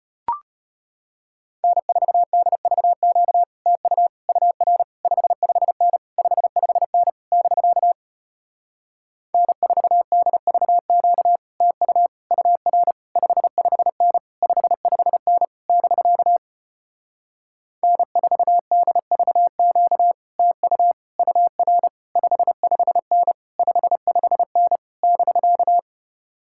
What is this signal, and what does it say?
Morse code, heard as a keyed tone, about 33 wpm, 700 Hz. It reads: N4DVQ TU UR 55N 55N <BK> N4DVQ TU UR 55N 55N <BK> N4DVQ TU UR 55N 55N <BK>